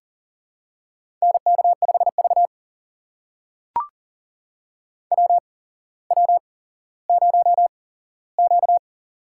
Morse code, heard as a keyed tone, about 42 wpm, spaced as for 17 wpm, 700 Hz. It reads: NK5V WW0Q
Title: USA Call Signs - 8x Character Spacing - 40wpm